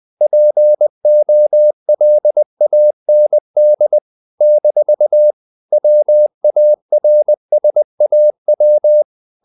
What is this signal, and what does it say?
Morse code, heard as a keyed tone, about 20 wpm, 600 Hz. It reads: POLAND - WARSAW
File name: CW / Practice / NationalCapitals20